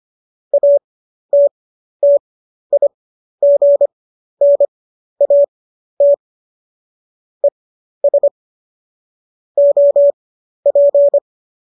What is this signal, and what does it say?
Morse code, heard as a keyed tone, about 25 wpm, spaced as for 12 wpm, 570 Hz. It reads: ATTIGNAT ES OP
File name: qso009 12